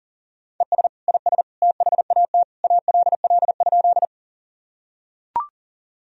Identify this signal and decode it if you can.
Morse code, heard as a keyed tone, about 40 wpm, 700 Hz. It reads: ES IS THAT ALL?
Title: Sentences from Top 100 Words and Abbreviations 40wpm